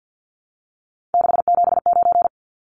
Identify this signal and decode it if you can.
Morse code, heard as a keyed tone, about 50 wpm, 700 Hz. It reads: 679
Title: Numbers - 3-Digit 50wpm